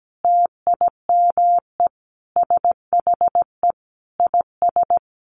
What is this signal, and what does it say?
Morse code, heard as a keyed tone, about 17 wpm, 700 Hz. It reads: TIME SHE IS
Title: Sentences from Top 700 Words 17wpm